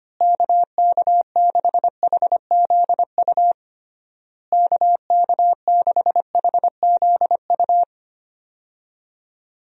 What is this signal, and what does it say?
Morse code, heard as a keyed tone, about 25 wpm, 700 Hz. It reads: KK6HZU KK6HZU
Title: USA Call Signs - 6 Characters - Repeated 3x - 25wpm